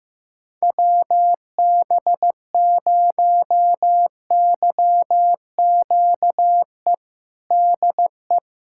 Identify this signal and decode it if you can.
Morse code, heard as a keyed tone, about 15 wpm, 700 Hz. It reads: WB0YQE DE
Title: USA Callsign DE Callsign - 6 Characters - 15wpm